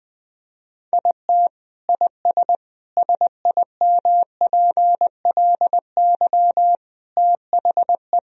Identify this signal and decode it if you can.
Morse code, heard as a keyed tone, about 21 wpm, 700 Hz. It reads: IT IS SIMPLY THE
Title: Sentences from Top 1000 Words - Review Part 4 - 20wpm